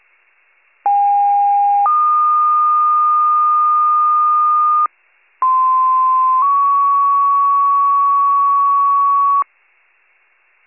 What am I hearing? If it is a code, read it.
Two-tone page, heard as a one-second tone, then a three-second tone, about 810 then 1220 Hz, about 1040 then 1120 Hz. It is heard from 0.9 to 9.4 s.